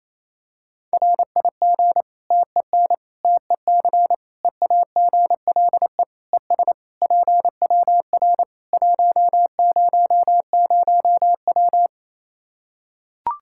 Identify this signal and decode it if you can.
Morse code, heard as a keyed tone, about 28 wpm, 700 Hz. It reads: RIG TEN TEC EAGLE ES PWR 100W
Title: Practice QSO - Exchange 2 - Info - 28wpm